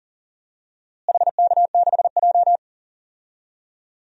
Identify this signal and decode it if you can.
Morse code, heard as a keyed tone, about 40 wpm, 700 Hz. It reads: HK6J
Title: DXCC-100 Call Signs - 4 Characters - Repeated 3x - 40wpm